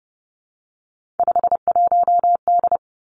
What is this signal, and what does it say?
Morse code, heard as a keyed tone, about 32 wpm, 700 Hz. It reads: 51D
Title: Three Challenging Letter-Number Combinations 30wpm